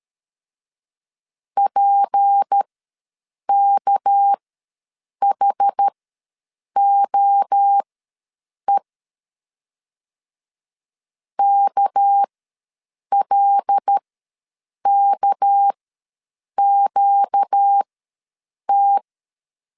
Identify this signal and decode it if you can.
Morse code, heard as a keyed tone, about 13 wpm, spaced as for 7 wpm, 800 Hz. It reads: PKHOE KLKQT
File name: LEZIONI / 6 PKX0 / 6 PK ws 12wpm 3116